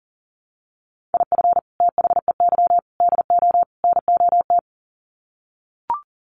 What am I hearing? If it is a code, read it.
Morse code, heard as a keyed tone, about 40 wpm, 700 Hz. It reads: IF THEY DO NOT
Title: Sets of 4 Words 40wpm